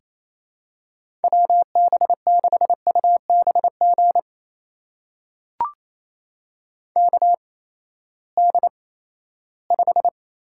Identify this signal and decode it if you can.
Morse code, heard as a keyed tone, about 28 wpm, spaced as for 16 wpm, 700 Hz. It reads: WB6UBG KD5
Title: USA Call Signs - 6 Characters - 8x Character Spacing - 28wpm